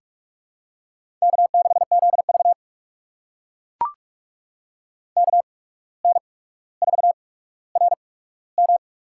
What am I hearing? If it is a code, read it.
Morse code, heard as a keyed tone, about 44 wpm, spaced as for 18 wpm, 700 Hz. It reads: K6ZV KN4RM
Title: USA Call Signs - 8x Character Spacing - 45wpm